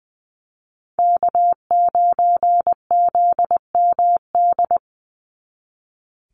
Morse code, heard as a keyed tone, about 20 wpm, 700 Hz. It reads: K9ZMD
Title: NA SOTA Chaser Callsigns 20wpm